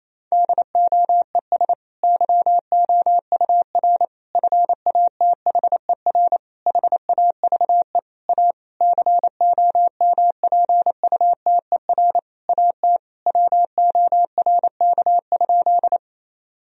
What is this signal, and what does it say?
Morse code, heard as a keyed tone, about 28 wpm, 700 Hz. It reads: DOES YOUR FATHER HAVE A COMPUTER AT WORK?
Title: Sentences from Top 1000 Words - Review Part 3 - 28wpm